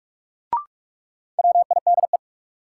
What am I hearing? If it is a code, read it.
Morse code, heard as a keyed tone, about 42 wpm, 700 Hz. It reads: WIDE
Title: Sentences from Top 900 Words - New Word Drill - 45wpm